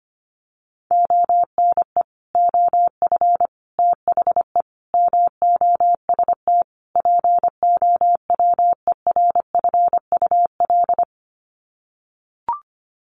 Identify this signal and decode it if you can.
Morse code, heard as a keyed tone, about 25 wpm, 700 Hz. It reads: ONE OF THE MOST POWERFUL
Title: Sets of 5 Words Encore 25wpm